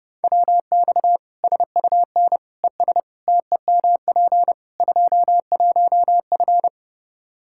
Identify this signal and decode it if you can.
Morse code, heard as a keyed tone, about 30 wpm, 700 Hz. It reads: WX SUN ES TEMP 21F
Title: Practice QSO - Exchange 2 - Info - 30wpm